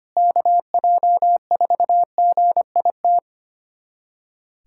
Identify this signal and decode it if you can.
Morse code, heard as a keyed tone, about 25 wpm, 700 Hz. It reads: KJ4GIT
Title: USA Callsign DE Callsign - 6 Characters - 25wpm